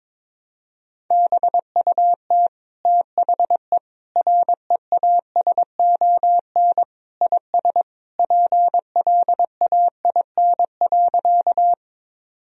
Morse code, heard as a keyed tone, about 22 wpm, 700 Hz. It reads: BUT THE REASON IS PLAIN.
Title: Sentences from Top 1000 Words - Review Part 2 - 22wpm